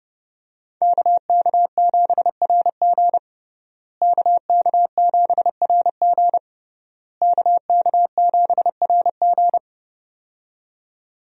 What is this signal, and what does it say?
Morse code, heard as a keyed tone, about 28 wpm, 700 Hz. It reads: KK7RG KK7RG KK7RG
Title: USA Call Signs - 5 Characters - Repeated 3x - 30wpm